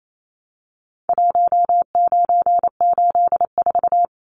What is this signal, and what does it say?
Morse code, heard as a keyed tone, about 28 wpm, 700 Hz. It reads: 1984
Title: Numbers - Contest Style - 4-Digit 28wpm